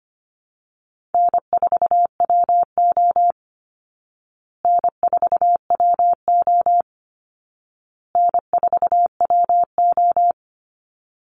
Morse code, heard as a keyed tone, about 25 wpm, 700 Hz. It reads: N4WO N4WO N4WO
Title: Call Signs Repeated 2x - 25wpm